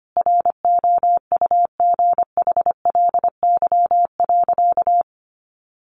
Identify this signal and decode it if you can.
Morse code, heard as a keyed tone, about 25 wpm, 700 Hz. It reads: ROUGHLY.
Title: Sentences from Top 1500 Words 25wpm